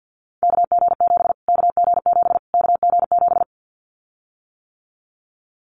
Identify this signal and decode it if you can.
Morse code, heard as a keyed tone, about 50 wpm, 700 Hz. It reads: KG7 KG7 KG7
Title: Callsign Prefixes for Top 25 DXCC - Repeated 3x - 50wpm